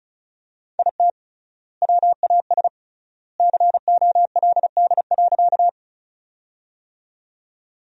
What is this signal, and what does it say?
Morse code, heard as a keyed tone, about 36 wpm, 700 Hz. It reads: IT WAS COLD.
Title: 3x Word-Spacing - Sentences from Top 400 Words - 35wpm